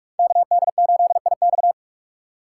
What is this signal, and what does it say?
Morse code, heard as a keyed tone, about 42 wpm, 690 Hz. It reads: KD8IX
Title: USA Call Signs - 4x Character Spacing - 45wpm